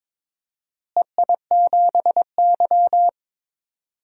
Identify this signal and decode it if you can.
Morse code, heard as a keyed tone, about 21 wpm, 700 Hz. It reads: EI7Y
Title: DXCC-100 Call Signs - 4 Characters - Repeated 2x - 22wpm